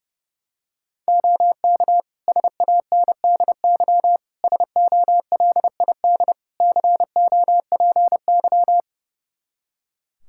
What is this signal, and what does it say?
Morse code, heard as a keyed tone, about 30 wpm, 700 Hz. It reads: OK SANDY SOLID COPY
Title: K1USN SST QSO Practice 30wpm